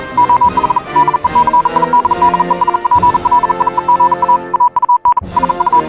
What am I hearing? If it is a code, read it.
Morse code, heard as a keyed tone, about 40 wpm, 980 Hz, with noise. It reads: OH DECIDDES PLIEGA AUN SU
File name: morse hymne(1)